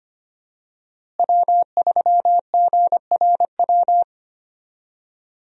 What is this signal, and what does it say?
Morse code, heard as a keyed tone, about 25 wpm, 700 Hz. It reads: W3GRW
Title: USA Call Signs - 5 Characters - Repeated 2x - 25wpm